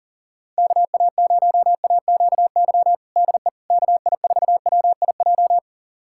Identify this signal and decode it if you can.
Morse code, heard as a keyed tone, about 40 wpm, 700 Hz. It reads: KA0AQY DE KI4WIJ
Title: USA Callsign DE Callsign - 6 Characters - 40wpm